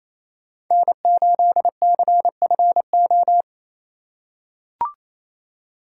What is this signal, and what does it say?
Morse code, heard as a keyed tone, about 27 wpm, 700 Hz. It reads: N8CFO
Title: USA Call Signs - 5 Characters - 4x Character Spacing - 28wpm